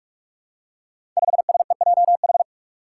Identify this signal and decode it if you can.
Morse code, heard as a keyed tone, about 43 wpm, 700 Hz. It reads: 5SEJH
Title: Five Challenging Letter-Number Combinations 45wpm